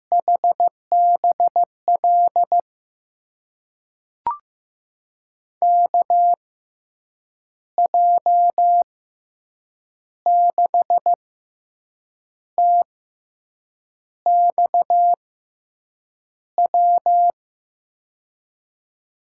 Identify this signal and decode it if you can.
Morse code, heard as a keyed tone, about 16 wpm, spaced as for 6 wpm, 700 Hz. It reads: HBL KJ6TXW
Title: USA Call Signs - 6 Characters - 6x Character Spacing - 15wpm